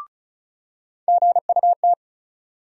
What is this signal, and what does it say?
Morse code, heard as a keyed tone, about 36 wpm, 700 Hz. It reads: GUT
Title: USA Call Sign Postfixes - 3 Letters 35wpm